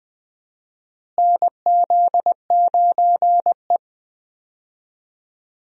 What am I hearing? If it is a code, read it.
Morse code, heard as a keyed tone, about 21 wpm, 700 Hz. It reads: NZ9E
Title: USA Call Signs - 4 Characters - Repeated 3x - 20wpm